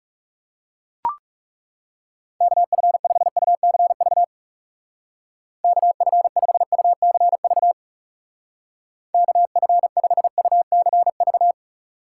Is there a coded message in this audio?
Yes: Morse code from 2.4 to 11.5 s, heard as a keyed tone, about 38 wpm, 700 Hz. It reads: KF5UCV KF5UCV KF5UCV